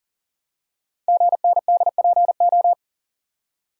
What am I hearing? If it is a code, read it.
Morse code, heard as a keyed tone, about 38 wpm, 690 Hz. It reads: GNDPO
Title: Code Groups - 5 Letters - 40wpm